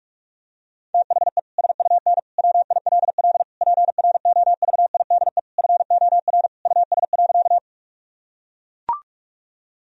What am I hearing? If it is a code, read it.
Morse code, heard as a keyed tone, about 41 wpm, 700 Hz. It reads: THE SUN WILL PROVIDE FOR US.
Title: Sentences from Top 1000 Words - Review Part 3 - 45wpm